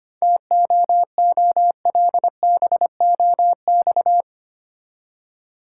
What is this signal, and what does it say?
Morse code, heard as a keyed tone, about 25 wpm, 690 Hz. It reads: TOOLBOX
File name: practice / 25wpm / Easy Words with Hints - 25wpm